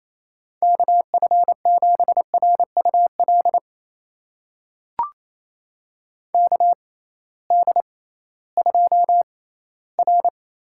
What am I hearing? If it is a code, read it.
Morse code, heard as a keyed tone, about 28 wpm, spaced as for 16 wpm, 700 Hz. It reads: KF7RUL KD2R
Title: USA Call Signs - 6 Characters - 6x Character Spacing - 28wpm